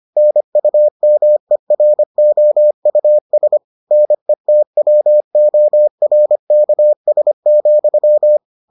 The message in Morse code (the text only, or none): NUMEROUS NETWORKS,